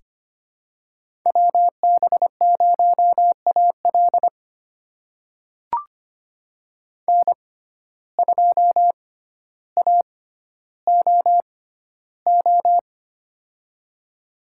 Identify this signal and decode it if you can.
Morse code, heard as a keyed tone, about 25 wpm, spaced as for 13 wpm, 700 Hz. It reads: WB0AL N2AOO
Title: USA Call Signs - 5 Characters - 6x Character Spacing - 25wpm